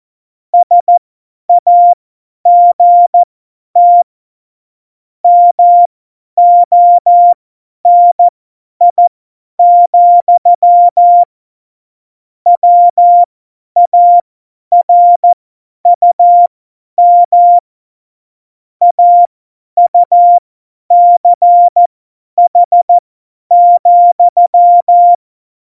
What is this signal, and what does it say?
Morse code, heard as a keyed tone, about 14 wpm, spaced as for 10 wpm, 700 Hz. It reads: SAGT MONI, WARUM AUCH,
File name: Morsekurs / Witze / Witz 19. November 2019, 50 F70